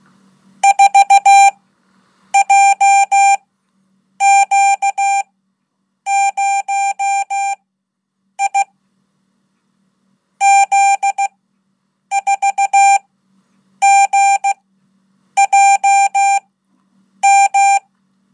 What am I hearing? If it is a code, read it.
Morse code, heard as a keyed tone, about 16 wpm, spaced as for 8 wpm, 790 Hz. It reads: 4JQ0I Z4GJM
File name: Code / 5min-1